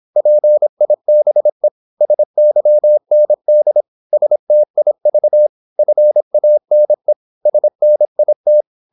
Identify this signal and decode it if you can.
Morse code, heard as a keyed tone, about 26 wpm, 600 Hz. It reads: PIBE SYND STIV FANE SNIT
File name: Korte ord DK 26wpm